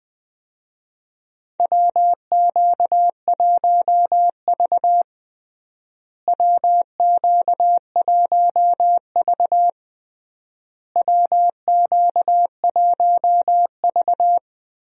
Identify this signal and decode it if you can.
Morse code, heard as a keyed tone, about 20 wpm, 700 Hz. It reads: WQ1V WQ1V WQ1V
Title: USA Call Signs - 4 Characters - Repeated 3x - 20wpm